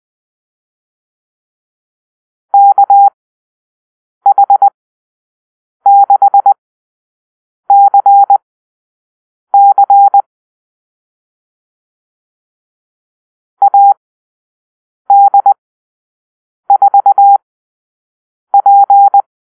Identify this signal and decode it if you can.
Morse code, heard as a keyed tone, about 20 wpm, spaced as for 6 wpm, 800 Hz. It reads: KH6CC AD4P
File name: arrl / Call-Sign Drill-6